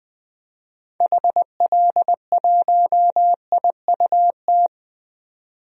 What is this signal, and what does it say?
Morse code, heard as a keyed tone, about 20 wpm, 700 Hz. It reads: HL1IUT
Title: DXCC-100 Call Signs - 6 Characters - 20wpm